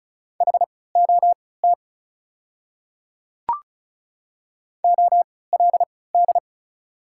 Morse code, heard as a keyed tone, about 34 wpm, spaced as for 20 wpm, 700 Hz. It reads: HOT OLD
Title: 3x Character Spacing - Top 300 Words - 35wpm